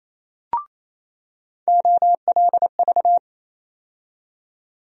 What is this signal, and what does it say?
Morse code, heard as a keyed tone, about 28 wpm, 700 Hz. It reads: OLV
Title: USA Call Sign Postfixes - 3 Letters 28wpm